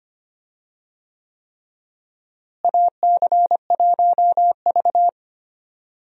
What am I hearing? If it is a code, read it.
Morse code, heard as a keyed tone, about 25 wpm, 700 Hz. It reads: AC1V
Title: USA Call Signs - 4 Characters - Repeated 3x - 25wpm